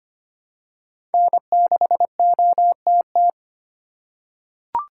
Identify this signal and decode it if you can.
Morse code, heard as a keyed tone, about 25 wpm, 700 Hz. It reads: N6OTT
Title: USA Call Signs - 5 Characters - 8x Character Spacing - 25wpm